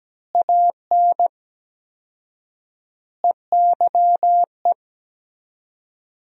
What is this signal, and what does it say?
Morse code, heard as a keyed tone, about 17 wpm, 700 Hz. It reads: AN EYE
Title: 4x Word-Spacing - Sentences from Top 300 Words - 17wpm